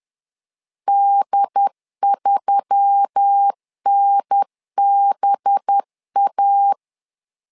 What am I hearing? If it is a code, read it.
Morse code, heard as a keyed tone, about 10 wpm, 800 Hz. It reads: D3NBA
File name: LEZIONI / 5 LBVC / 5 LBV 10wpm 2248